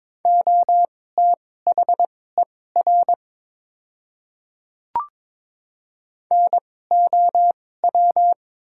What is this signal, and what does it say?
Morse code, heard as a keyed tone, about 21 wpm, spaced as for 16 wpm, 700 Hz. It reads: OTHER NOW
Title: 2x Character Spacing - Top 100 Words - 22wpm